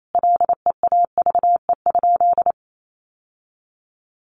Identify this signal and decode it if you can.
Morse code, heard as a keyed tone, about 28 wpm, 700 Hz. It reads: LEAVE?
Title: Sentences from Top 1100 Words 28wpm